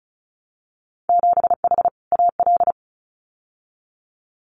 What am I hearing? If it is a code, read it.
Morse code, heard as a keyed tone, about 36 wpm, 700 Hz. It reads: 7H AL
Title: Winter Field Day Exchanges Encore 35wpm